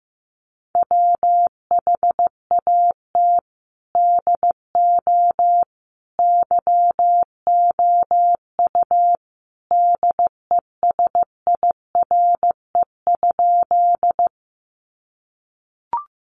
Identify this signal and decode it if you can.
Morse code, heard as a keyed tone, about 15 wpm, 700 Hz. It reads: WHAT DO YOU DESIRE?
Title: Sentences from Top 1100 Words 15wpm